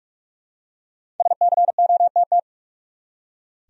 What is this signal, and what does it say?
Morse code, heard as a keyed tone, about 42 wpm, 700 Hz. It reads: SCOTT
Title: USA Common Names - Max 5 Letters - 45wpm